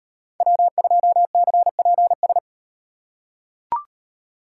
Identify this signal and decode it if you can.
Morse code, heard as a keyed tone, about 38 wpm, 700 Hz. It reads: W2CPS